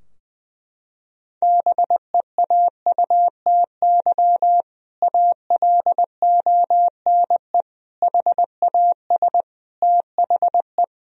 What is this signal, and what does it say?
Morse code, heard as a keyed tone, about 20 wpm, 700 Hz. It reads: BEAUTY ALONE HAS THE
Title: Sentences from Top 1000 Words - Review Part 2 - 20wpm